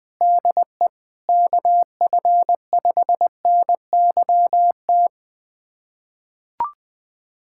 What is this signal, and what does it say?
Morse code, heard as a keyed tone, about 20 wpm, 700 Hz. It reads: DE KF5NYT
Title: USA Callsign DE Callsign - 6 Characters - 20wpm